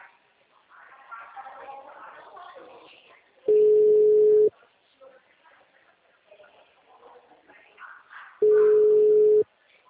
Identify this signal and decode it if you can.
Ringback tone (Europe, 425 Hz).
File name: out- -2014- - - .34617